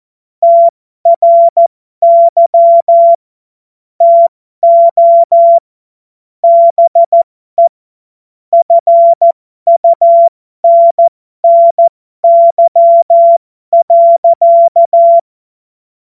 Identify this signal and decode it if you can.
Morse code, heard as a keyed tone, about 14 wpm, spaced as for 12 wpm, 680 Hz. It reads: TRY TO BE FUNNY.
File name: Morsekurs / Witze